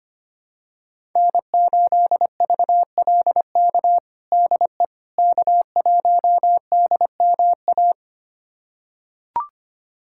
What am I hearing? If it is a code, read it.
Morse code, heard as a keyed tone, about 25 wpm, 700 Hz. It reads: N8VLK DE K1DMA
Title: USA Callsign DE Callsign - 5 Characters - 25wpm